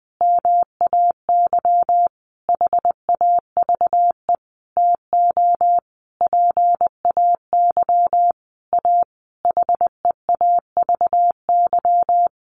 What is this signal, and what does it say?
Morse code, heard as a keyed tone, about 20 wpm, 700 Hz. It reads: MAY HAVE TO PAY A HEAVY
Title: Sentences from Top 1100 Words 20wpm